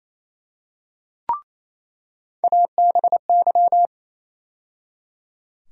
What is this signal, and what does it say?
Morse code, heard as a keyed tone, about 28 wpm, 700 Hz. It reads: ABY
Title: USA Call Sign Postfixes - 3 Letters 28wpm